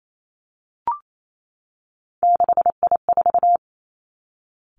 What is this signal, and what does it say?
Morse code, heard as a keyed tone, about 28 wpm, 700 Hz. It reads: 6I4
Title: Three Challenging Letter-Number Combinations 28wpm